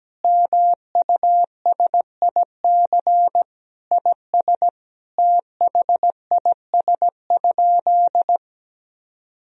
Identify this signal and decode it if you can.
Morse code, heard as a keyed tone, about 17 wpm, 700 Hz. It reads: MUSIC IS THIS?
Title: Sentences from Top 1000 Words - Review Part 1 - 17wpm